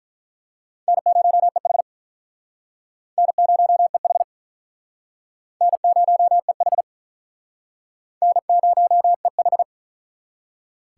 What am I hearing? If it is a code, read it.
Morse code, heard as a keyed tone, about 48 wpm, 700 Hz. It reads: N0EH N0EH N0EH N0EH